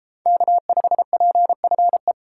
Morse code, heard as a keyed tone, about 32 wpm, 700 Hz. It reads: K5PFE